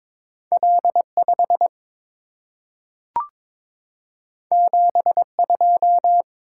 Morse code, heard as a keyed tone, about 22 wpm, 700 Hz. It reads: L5 72
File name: practice / 22wpm / Two Letter-Number Combinations - 22wpm